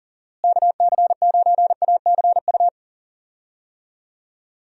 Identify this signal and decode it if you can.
Morse code, heard as a keyed tone, about 40 wpm, 700 Hz. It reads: KC9ACU
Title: USA Call Signs - 6 Characters - Repeated 3x - 40wpm